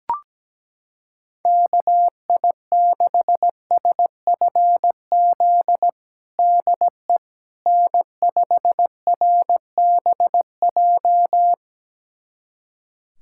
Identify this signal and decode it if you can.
Morse code, heard as a keyed tone, about 17 wpm, 700 Hz. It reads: KI6SFZ DE N5RBJ
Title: USA Callsign DE Callsign 17wpm